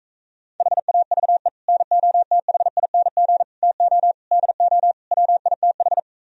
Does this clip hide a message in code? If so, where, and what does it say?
Yes: Morse code from 0.6 to 6.0 s, heard as a keyed tone, about 45 wpm, 700 Hz. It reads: HAVE NOTHING TO DO WITH